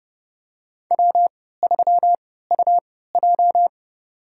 Morse code, heard as a keyed tone, about 32 wpm, spaced as for 17 wpm, 700 Hz. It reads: W3UJ
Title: USA Call Signs - 4 Characters - 3x Character Spacing - 30wpm